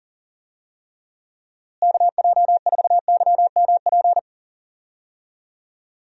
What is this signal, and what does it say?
Morse code, heard as a keyed tone, about 37 wpm, 700 Hz. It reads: KJ4YMP
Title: USA Call Signs - 6 Characters - Repeated 3x - 40wpm